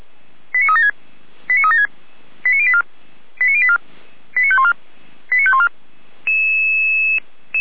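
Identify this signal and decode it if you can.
Five-tone selective call (ZVEI), heard as quick runs of five tones, tones of about 70 ms. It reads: 89376 89376 89084 89084 89514 89514
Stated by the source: ZVEI five-tone selective call 89115 received by SDR.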